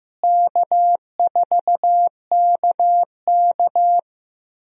Morse code, heard as a keyed tone, about 15 wpm, 700 Hz. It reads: K4KK
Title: USA Call Signs - 4 Characters - Repeated 3x - 15wpm